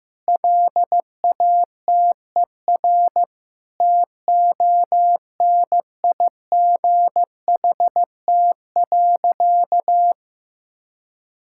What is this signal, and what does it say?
Morse code, heard as a keyed tone, about 15 wpm, 700 Hz. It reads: LATER TONIGHT.
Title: Sentences from Top 1000 Words - Review Part 6 - 15wpm